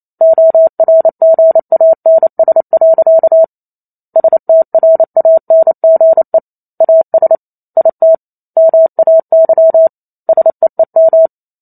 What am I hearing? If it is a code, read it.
Morse code, heard as a keyed tone, about 28 wpm, 650 Hz. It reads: ORGANS. STRANGE AS IT MAY SEEM